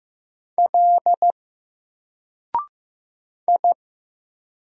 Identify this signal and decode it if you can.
Morse code, heard as a keyed tone, about 15 wpm, 700 Hz. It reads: L I
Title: Single Character (Letter:Number:Punctuation:Prosign) 15wpm